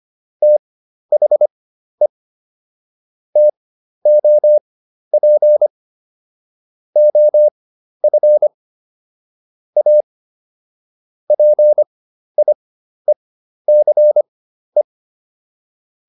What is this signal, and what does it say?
Morse code, heard as a keyed tone, about 25 wpm, spaced as for 12 wpm, 600 Hz. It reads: THE TOP OF A PIECE